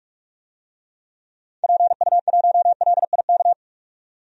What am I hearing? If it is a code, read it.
Morse code, heard as a keyed tone, about 47 wpm, 700 Hz. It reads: PU1LIK
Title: DXCC-100 Call Signs - 6 Characters - 45wpm